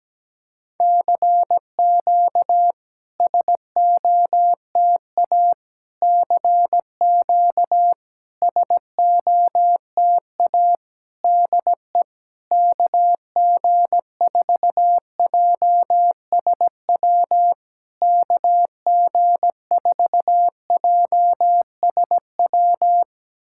Morse code, heard as a keyed tone, about 17 wpm, 700 Hz. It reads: CQ SOTA CQ SOTA DE KG4JSW KG4JSW
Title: SOTA QSO Practice 17wpm